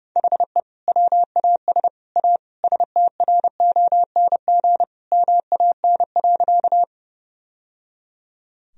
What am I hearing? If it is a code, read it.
Morse code, heard as a keyed tone, about 30 wpm, 700 Hz. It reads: HE WAS A STRONG MAN.
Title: Sentences from Top 1000 Words - Review Part 3 - 30wpm